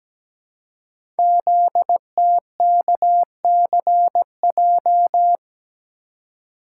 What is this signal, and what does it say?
Morse code, heard as a keyed tone, about 17 wpm, 700 Hz. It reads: ZTKCJ
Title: Code Groups - 5 Letters - 17wpm